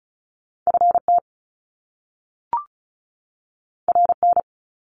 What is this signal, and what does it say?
Morse code, heard as a keyed tone, about 36 wpm, 700 Hz. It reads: FT RN
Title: Top Two Letter Combinations in Words 35wpm